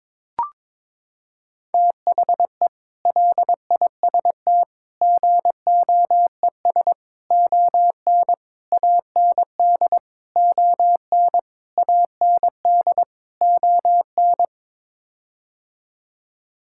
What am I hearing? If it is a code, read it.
Morse code, heard as a keyed tone, about 22 wpm, 700 Hz. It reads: THE LIST GOES ON AND ON AND ON
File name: practice / 22wpm / Sets of 8 Words - 22wpm